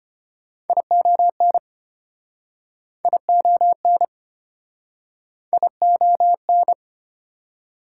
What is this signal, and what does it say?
Morse code, heard as a keyed tone, about 32 wpm, 700 Hz. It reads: ION ION ION